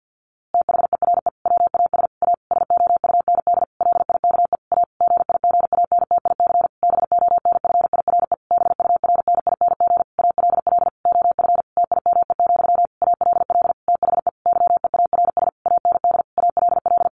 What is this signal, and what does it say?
Morse code, heard as a keyed tone, about 50 wpm, 700 Hz. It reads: THERE WAS A SOUND LIKE A GIGANTIC BONFIRE BURNING ALL OF TIME, ALL THE YEARS AND ALL
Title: A Sound Of Thunder 50wpm